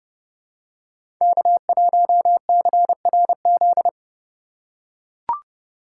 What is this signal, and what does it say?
Morse code, heard as a keyed tone, about 31 wpm, 700 Hz. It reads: K1CRZ
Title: USA Call Signs - 3x Character Spacing - 30wpm